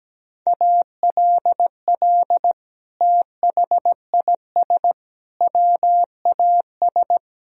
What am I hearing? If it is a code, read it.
Morse code, heard as a keyed tone, about 17 wpm, 700 Hz. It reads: ALL THIS WAS
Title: Sentences from Top 1000 Words - Review Part 2 - 17wpm